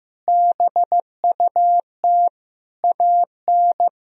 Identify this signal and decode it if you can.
Morse code, heard as a keyed tone, about 16 wpm, 700 Hz. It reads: BUT AN
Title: Sentences from Top 1000 Words - Review Part 3 - 15wpm